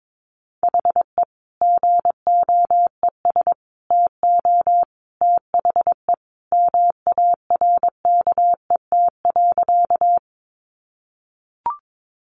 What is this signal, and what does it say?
Morse code, heard as a keyed tone, about 22 wpm, 700 Hz. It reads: HE GOES TO THE MARKET.
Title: Additional Easy Phrases 22wpm